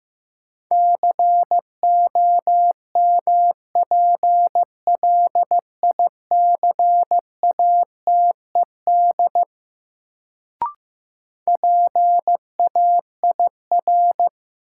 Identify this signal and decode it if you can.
Morse code, heard as a keyed tone, about 15 wpm, 700 Hz. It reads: COMPLICATED PAIR
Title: Sentences from Top 1200 Words - New Word Drill Alt - 15wpm